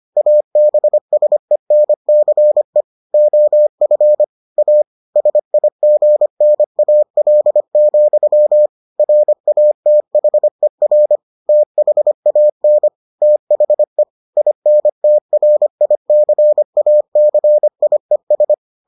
Morse code, heard as a keyed tone, about 25 wpm, 600 Hz. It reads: ABSENCE OF A SIGNAL, RATHER THAN THE INTRICACIES